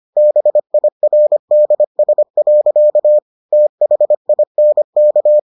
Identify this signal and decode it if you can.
Morse code, heard as a keyed tone, about 25 wpm, 600 Hz. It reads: BIRDS. THINK